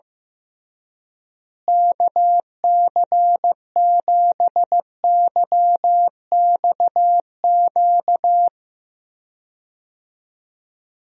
Morse code, heard as a keyed tone, about 16 wpm, 700 Hz. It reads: KC7YXQ
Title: USA Call Signs - 6 Characters - Repeated 3x - 15wpm